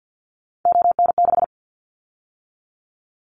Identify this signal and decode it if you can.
Morse code, heard as a keyed tone, about 50 wpm, 700 Hz. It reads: ON6
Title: Callsign Prefixes for Top 50 DXCC - 50wpm